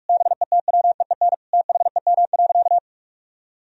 Morse code, heard as a keyed tone, about 46 wpm, 700 Hz. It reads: BETWEEN THEM.
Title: Sentences from Top 1000 Words - Review Part 3 - 45wpm